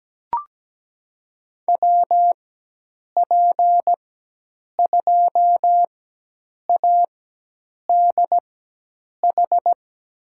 Morse code, heard as a keyed tone, about 17 wpm, spaced as for 8 wpm, 700 Hz. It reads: WP2ADH
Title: USA Call Signs - 6 Characters - 4x Character Spacing - 17wpm